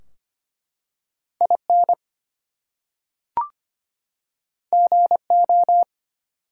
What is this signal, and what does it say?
Morse code, heard as a keyed tone, about 25 wpm, 700 Hz. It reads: IN GO
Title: Two Letter Words - 25wpm